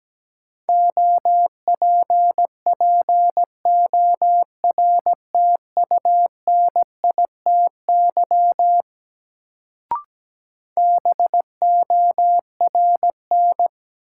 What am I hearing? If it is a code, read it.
Morse code, heard as a keyed tone, about 17 wpm, 700 Hz. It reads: OPPORTUNITY BORN
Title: Sentences from Top 1000 Words - New Word Drill - 17wpm